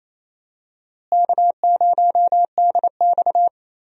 Morse code, heard as a keyed tone, about 28 wpm, 700 Hz. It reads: K0DX
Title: USA Call Signs - 4 Characters - Repeated 2x - 28wpm